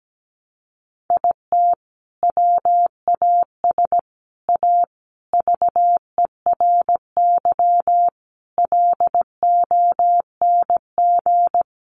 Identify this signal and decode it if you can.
Morse code, heard as a keyed tone, about 17 wpm, 700 Hz. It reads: IT WAS A VERY LONG
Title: Sentences from Top 1200 Words 17wpm